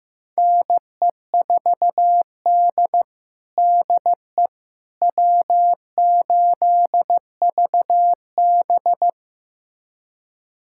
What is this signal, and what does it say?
Morse code, heard as a keyed tone, about 15 wpm, 700 Hz. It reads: NE4D DE W8VB
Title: USA Callsign DE Callsign - 4 Characters - 15wpm